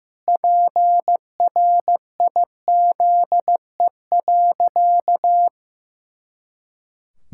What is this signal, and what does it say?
Morse code, heard as a keyed tone, about 15 wpm, 700 Hz. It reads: PRIZE.
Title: Pangram Challenge v2 - 15wpm